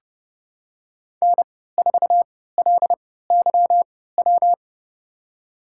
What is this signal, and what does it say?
Morse code, heard as a keyed tone, about 29 wpm, spaced as for 17 wpm, 700 Hz. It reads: N4LYW
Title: USA Call Signs - 5 Characters - 3x Character Spacing - 30wpm